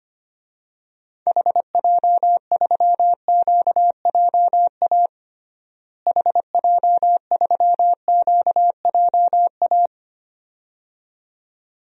Morse code, heard as a keyed tone, about 25 wpm, 700 Hz. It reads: HJ3QJA HJ3QJA
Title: DXCC-100 Call Signs - 6 Characters - Repeated 2x - 25wpm